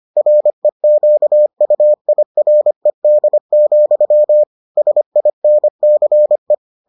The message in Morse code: REQUIRED, SINCE

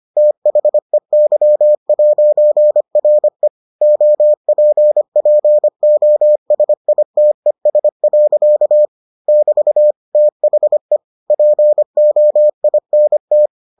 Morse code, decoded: THEY'RE OPPOSITES. <BT> THE POINT

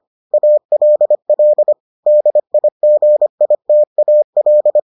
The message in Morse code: ALL DIGITAL

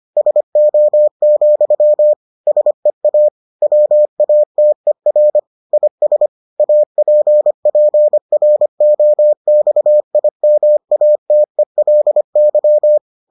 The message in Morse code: SO, SEA WATER IS APPROXIMATELY

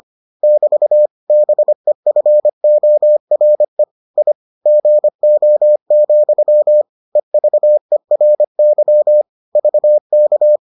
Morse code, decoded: <BT> BEFORE I GO, EVERY VK

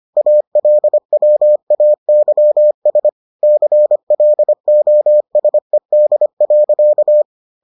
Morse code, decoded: ALWAYS CLOSED.

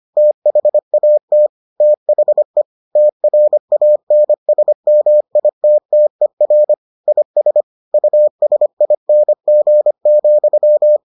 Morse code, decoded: THAT THE TRANSMITTER IS USING,

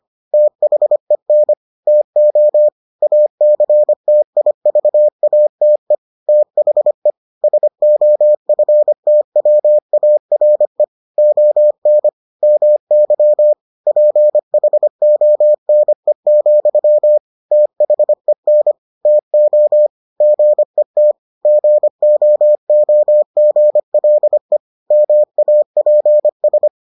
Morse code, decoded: THEN TO ACTIVATE THE SOFTWARE ON MY PHONE, THEN TO GET GOOGLE MAPS

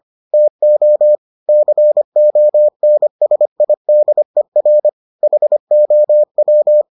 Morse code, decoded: TO CONSIDER HOW

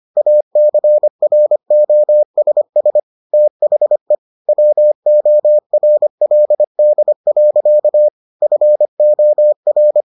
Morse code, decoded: ACROSS THE WORLD. FOR

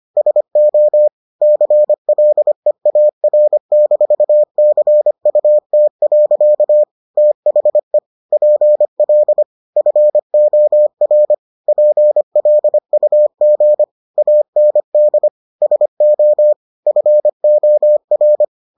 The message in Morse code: SO CLEAR-CUT. THE PL FOR PLUG AND SO FOR

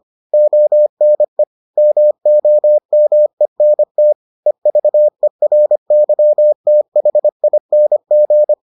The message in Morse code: ONE MOMENT EVERYTHING

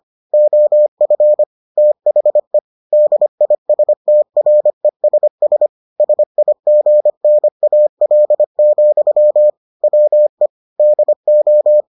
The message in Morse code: OF THE DISTRESS SIGNAL, WE DO

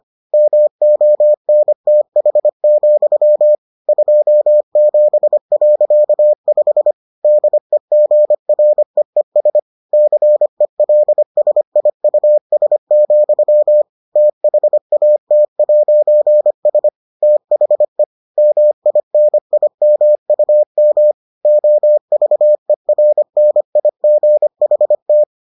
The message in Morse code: MONTH, 27.5 DEGREES CELSIUS, THAT'S THE MINIMUM OVERNIGHT

